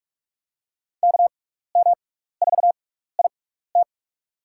Morse code, heard as a keyed tone, about 46 wpm, spaced as for 16 wpm, 700 Hz. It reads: KM4IT
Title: USA Call Signs - 5 Characters - 6x Character Spacing - 45wpm